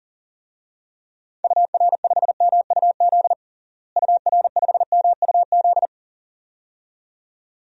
Morse code, heard as a keyed tone, about 40 wpm, 700 Hz. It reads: UR5MUZ UR5MUZ
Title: DXCC-100 Call Signs - 6 Characters - Repeated 2x - 40wpm